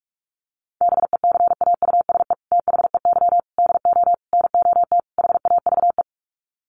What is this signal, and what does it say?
Morse code, heard as a keyed tone, about 44 wpm, 700 Hz. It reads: BECAUSE THEY DO NOT HAVE
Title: Sets of 5 Words Encore2 45wpm